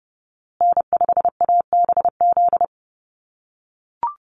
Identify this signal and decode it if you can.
Morse code, heard as a keyed tone, about 30 wpm, 700 Hz. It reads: N5ABZ
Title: USA Call Signs Repeated 2x 30wpm